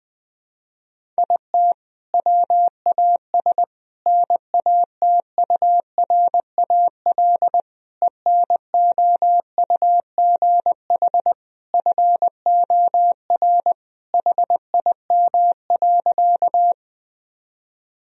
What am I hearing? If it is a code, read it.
Morse code, heard as a keyed tone, about 20 wpm, 700 Hz. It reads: IT WAS NATURAL ENOUGH FOR HIM.
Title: Sentences from Top 1000 Words - Review Part 4 - 20wpm